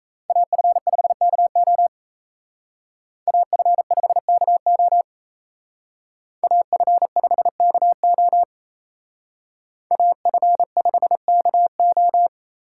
Morse code, heard as a keyed tone, about 34 wpm, 700 Hz. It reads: AF5KO AF5KO AF5KO AF5KO
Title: Speed Racing - USA 2x2 Call Signs 28wpm